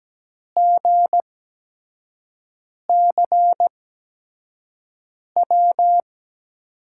Morse code, heard as a keyed tone, about 17 wpm, spaced as for 5 wpm, 700 Hz. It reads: GCW
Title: USA Call Signs - 6 Characters - 8x Character Spacing - 17wpm